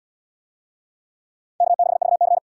Morse code, heard as a keyed tone, about 74 wpm, 700 Hz. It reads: K6VC